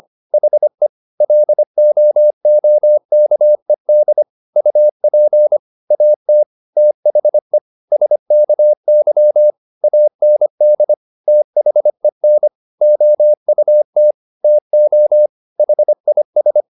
Morse code, heard as a keyed tone, about 25 wpm, 600 Hz. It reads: HE LOOKED UP AT THE SKY AND THEN OUT TO HIS